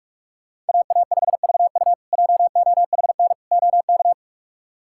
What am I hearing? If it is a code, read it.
Morse code, heard as a keyed tone, about 46 wpm, 700 Hz. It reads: AA5VU JOHN OK
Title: K1USN SST QSO Practice - 45wpm